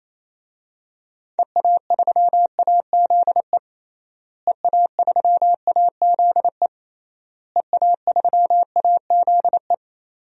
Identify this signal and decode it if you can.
Morse code, heard as a keyed tone, about 28 wpm, 700 Hz. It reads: EA3AZE EA3AZE EA3AZE
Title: DXCC-100 Call Signs - 6 Characters - Repeated 3x - 28wpm